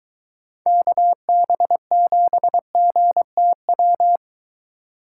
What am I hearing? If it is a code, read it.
Morse code, heard as a keyed tone, about 23 wpm, 700 Hz. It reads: KB7GTW